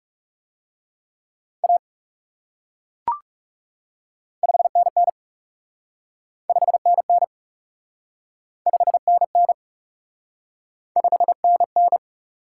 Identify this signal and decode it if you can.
Morse code, heard as a keyed tone, about 37 wpm, 700 Hz. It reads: A 5NN 5NN 5NN 5NN
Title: Speed-Racing Top 180 QSO Elements 30wpm